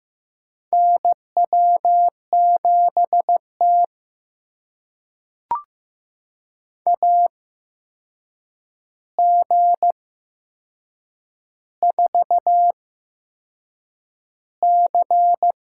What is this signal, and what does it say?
Morse code, heard as a keyed tone, about 15 wpm, spaced as for 6 wpm, 700 Hz. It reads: NW7T AG4C
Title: USA Call Signs - 4 Characters - 8x Character Spacing - 15wpm